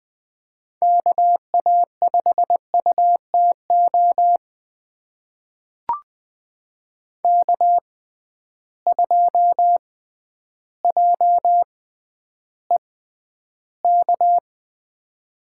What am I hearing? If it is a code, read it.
Morse code, heard as a keyed tone, about 20 wpm, spaced as for 11 wpm, 700 Hz. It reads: KA5UTO K2JEK
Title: USA Call Signs - 6x Character Spacing - 20wpm